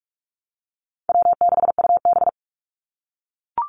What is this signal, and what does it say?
Morse code, heard as a keyed tone, about 44 wpm, 700 Hz. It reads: W6UB
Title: NA SOTA Activator Callsigns 45wpm